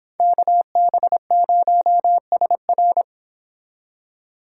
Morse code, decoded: KB0SR

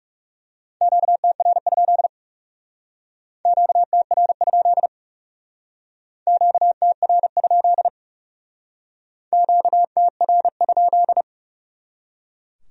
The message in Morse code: QTR? QTR? QTR? QTR?

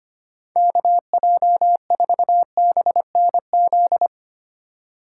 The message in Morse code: KJ4BNZ

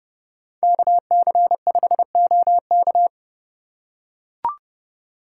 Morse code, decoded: KC5OK

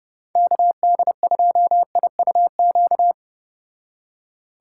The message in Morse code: KD2IUQ

